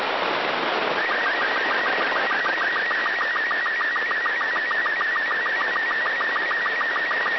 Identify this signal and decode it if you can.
SSTV (slow-scan television), Robot 36.
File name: ao-51-apollo-11